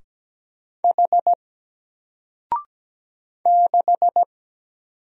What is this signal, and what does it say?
Morse code, heard as a keyed tone, about 17 wpm, 700 Hz. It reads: H 6
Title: Single Letter-Number 17wpm